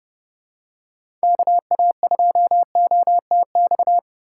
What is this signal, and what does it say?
Morse code, heard as a keyed tone, about 29 wpm, 700 Hz. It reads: KA2OTX
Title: USA Call Signs - 2x Character Spacing - 30wpm